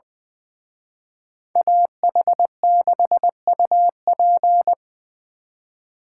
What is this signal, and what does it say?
Morse code, heard as a keyed tone, about 24 wpm, 700 Hz. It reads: AH6UP
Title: Speed Racing - USA 2x2 Call Signs 20wpm